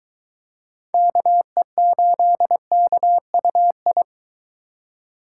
Morse code, heard as a keyed tone, about 23 wpm, 700 Hz. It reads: KE8KUI